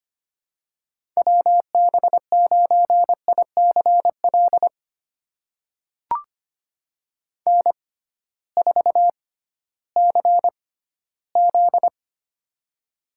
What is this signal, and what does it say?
Morse code, heard as a keyed tone, about 25 wpm, spaced as for 15 wpm, 700 Hz. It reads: WB9ICL N4CZ
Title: USA Call Signs - 6x Character Spacing - 25wpm